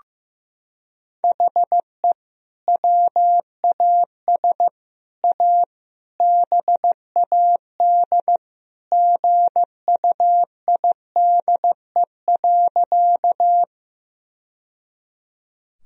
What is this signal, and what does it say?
Morse code, heard as a keyed tone, about 15 wpm, 700 Hz. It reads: HE WAS A BAD GUIDE.
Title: Sentences from Top 1000 Words - Review Part 3 - 15wpm